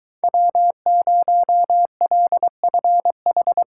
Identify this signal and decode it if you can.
Morse code, heard as a keyed tone, about 23 wpm, 700 Hz. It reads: W0LFH